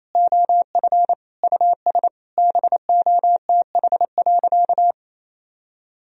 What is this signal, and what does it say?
Morse code, heard as a keyed tone, about 28 wpm, 700 Hz. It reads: OF US BOTH.
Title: Sentences from Top 1000 Words - Review Part 4 - 28wpm